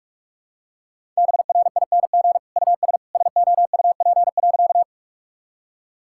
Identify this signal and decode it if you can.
Morse code, heard as a keyed tone, about 46 wpm, 700 Hz. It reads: BRING US SOUP.
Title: Sentences from Top 1000 Words - Review Part 6 - 45wpm